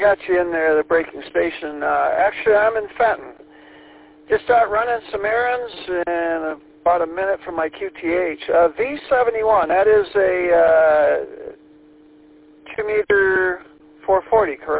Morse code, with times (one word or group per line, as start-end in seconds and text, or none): none